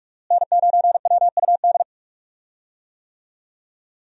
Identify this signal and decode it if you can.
Morse code, heard as a keyed tone, about 47 wpm, 700 Hz. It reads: N9WUD
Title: USA Call Signs - 5 Characters - Repeated 3x - 45wpm